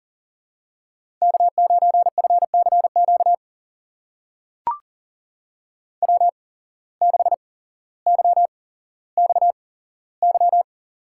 Morse code, heard as a keyed tone, about 40 wpm, spaced as for 17 wpm, 700 Hz. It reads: K9FCQ W6YXY
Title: USA Call Signs - 5 Characters - 8x Character Spacing - 40wpm